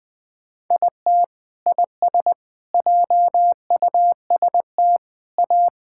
Morse code, heard as a keyed tone, about 21 wpm, 700 Hz. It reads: IT IS JUST A
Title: Sentences with 2-Word Phrases - 20wpm